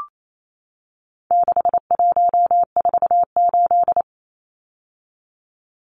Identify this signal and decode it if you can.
Morse code, heard as a keyed tone, about 28 wpm, 700 Hz. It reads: 6148
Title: Numbers - 4-Digit 28wpm